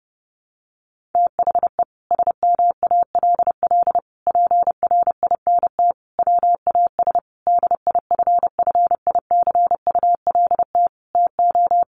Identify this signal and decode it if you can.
Morse code, heard as a keyed tone, about 30 wpm, 700 Hz. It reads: THE SMALL PRINT WAS DIFFICULT TO
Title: Sentences from Top 1400 Words 30wpm